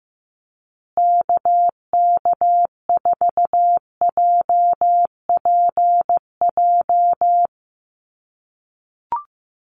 Morse code, heard as a keyed tone, about 15 wpm, 700 Hz. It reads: KK4JPJ
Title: USA Call Signs Repeated 2x 15wpm